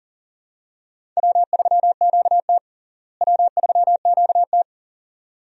Morse code, heard as a keyed tone, about 37 wpm, 700 Hz. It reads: W3QT W3QT
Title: USA Call Signs - 4 Characters - Repeated 2x - 40wpm